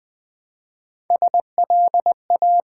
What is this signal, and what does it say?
Morse code, heard as a keyed tone, about 20 wpm, 700 Hz. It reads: SLA
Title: Build A Word - 1 Letter At A Time - 5-Letter Words - 20wpm